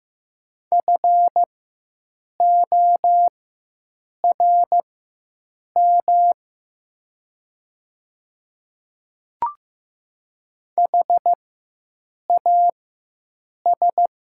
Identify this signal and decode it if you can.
Morse code, heard as a keyed tone, about 15 wpm, spaced as for 7 wpm, 700 Hz. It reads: FORM HAS
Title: 4x Character Spacing - Top 200 Words - 15wpm